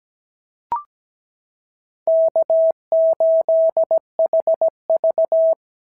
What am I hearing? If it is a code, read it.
Morse code, heard as a keyed tone, about 17 wpm, 690 Hz. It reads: K8HV
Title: SOTA QSO Practice 17wpm